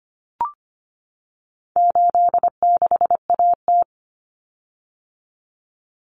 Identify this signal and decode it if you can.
Morse code, heard as a keyed tone, about 25 wpm, 700 Hz. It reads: 86AT
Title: Common Cut Numbers - 4-Digit 25wpm